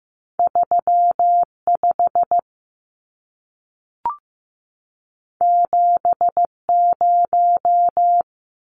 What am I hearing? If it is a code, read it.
Morse code, heard as a keyed tone, about 15 wpm, 700 Hz. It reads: 35 70
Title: Numbers - 2-Digit 15wpm